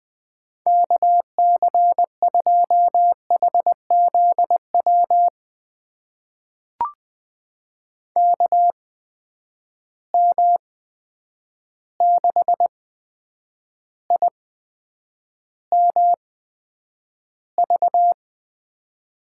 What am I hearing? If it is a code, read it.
Morse code, heard as a keyed tone, about 20 wpm, spaced as for 9 wpm, 700 Hz. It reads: KC2HZW KM6IMV
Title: USA Call Signs - 6 Characters - 8x Character Spacing - 20wpm